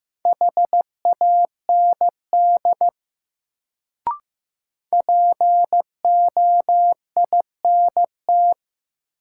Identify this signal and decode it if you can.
Morse code, heard as a keyed tone, about 15 wpm, 700 Hz. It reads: HAND POINT
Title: Sentences from Top 200 Words - New Word Drill - 15wpm